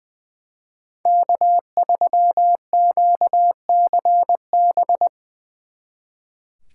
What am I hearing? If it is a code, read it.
Morse code, heard as a keyed tone, about 23 wpm, 700 Hz. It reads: K3QCB
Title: Speed Racing - USA Call Signs 20wpm